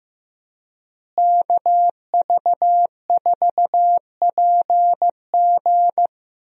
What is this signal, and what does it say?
Morse code, heard as a keyed tone, about 15 wpm, 700 Hz. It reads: KV4PG
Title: USA Call Signs - 5 Characters - Repeated 3x - 15wpm